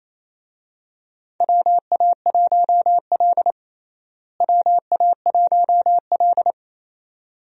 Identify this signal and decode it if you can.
Morse code, heard as a keyed tone, about 28 wpm, 700 Hz. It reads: WA1L WA1L
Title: USA Call Signs - 4 Characters - Repeated 2x - 28wpm